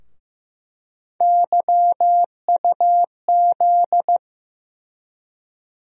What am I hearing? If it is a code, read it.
Morse code, heard as a keyed tone, about 15 wpm, 700 Hz. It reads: YUZ
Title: USA Call Sign Postfixes - 3 Letters 15wpm